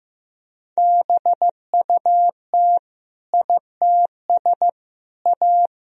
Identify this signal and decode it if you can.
Morse code, heard as a keyed tone, about 16 wpm, 700 Hz. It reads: BUT ITS A
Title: Sentences from Top 1000 Words - Review Part 2 - 15wpm